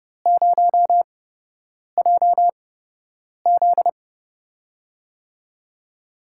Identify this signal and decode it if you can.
Morse code, heard as a keyed tone, about 30 wpm, spaced as for 8 wpm, 700 Hz. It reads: 0JZ
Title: USA Call Signs - 4 Characters - 8x Character Spacing - 30wpm